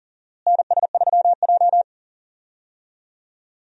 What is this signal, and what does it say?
Morse code, heard as a keyed tone, about 40 wpm, 700 Hz. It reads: NS3J
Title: USA Call Signs - 4 Characters - Repeated 3x - 40wpm